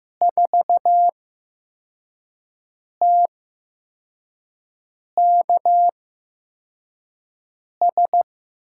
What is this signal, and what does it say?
Morse code, heard as a keyed tone, about 15 wpm, spaced as for 4 wpm, 700 Hz. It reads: 4TKS
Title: USA Call Signs - 6 Characters - 8x Character Spacing - 15wpm